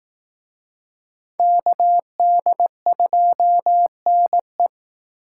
Morse code, heard as a keyed tone, about 18 wpm, 700 Hz. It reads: KD2NE
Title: Speed Racing - USA 2x2 Call Signs 15wpm